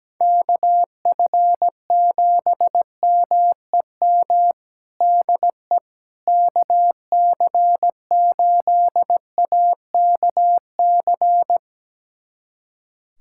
Morse code, heard as a keyed tone, about 17 wpm, 700 Hz. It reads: KF7MEM DE KC8AKC
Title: USA Callsign DE Callsign - 6 Characters - 17wpm